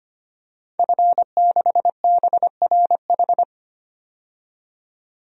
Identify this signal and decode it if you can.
Morse code, heard as a keyed tone, about 25 wpm, 700 Hz. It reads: F6BRH
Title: DXCC-100 Call Signs - 5 Characters - 25wpm